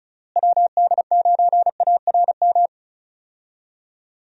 Morse code, heard as a keyed tone, about 36 wpm, 700 Hz. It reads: WD9ARM
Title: USA Call Signs - 6 Characters - Repeated 2x - 35wpm